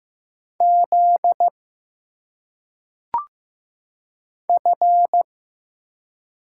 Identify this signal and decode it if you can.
Morse code, heard as a keyed tone, about 15 wpm, 700 Hz. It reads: Z F
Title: Single Letters - Phonetic Pronunciation 15wpm